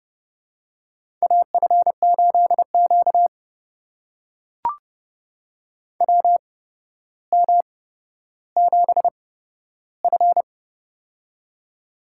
Morse code, heard as a keyed tone, about 32 wpm, spaced as for 13 wpm, 700 Hz. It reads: AF8Q WM7F
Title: USA Call Signs - 4 Characters - 8x Character Spacing - 30wpm